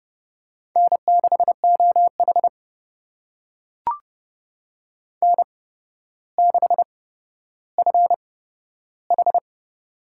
Morse code, heard as a keyed tone, about 29 wpm, spaced as for 13 wpm, 700 Hz. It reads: N6OH N6FH
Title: USA Call Signs - 8x Character Spacing - 30wpm